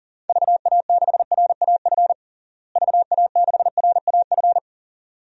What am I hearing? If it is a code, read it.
Morse code, heard as a keyed tone, about 40 wpm, 700 Hz. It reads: VA6RAF VA6RAF
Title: DXCC-100 Call Signs - 6 Characters - Repeated 2x - 40wpm